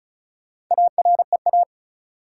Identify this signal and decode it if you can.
Morse code, heard as a keyed tone, about 36 wpm, 700 Hz. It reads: AREA
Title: Sentences from Top 1000 Words - Review Part 2 - New Word Drill - 35wpm